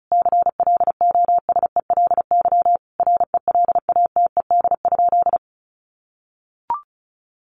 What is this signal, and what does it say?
Morse code, heard as a keyed tone, about 34 wpm, 700 Hz. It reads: CLOSELY RELATED?
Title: Sentences from Top 1400 Words 35wpm